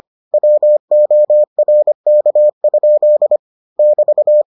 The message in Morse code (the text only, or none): WORK? <BT>